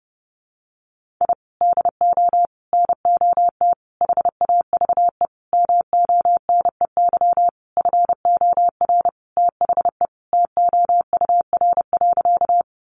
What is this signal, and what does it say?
Morse code, heard as a keyed tone, about 30 wpm, 700 Hz. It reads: I DO NOT HAVE MONEY FOR THE TOUR.
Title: Sentences from Top 1400 Words 30wpm